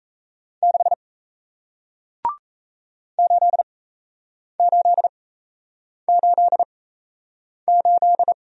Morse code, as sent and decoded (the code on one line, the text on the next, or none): -.... / ---.. ---.. ---.. ---..
6 8888